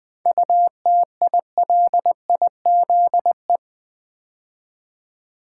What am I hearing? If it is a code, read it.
Morse code, heard as a keyed tone, about 20 wpm, 690 Hz. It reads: UTILIZE
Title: 7 Letter Words - First 3 Letters Given - 20wpm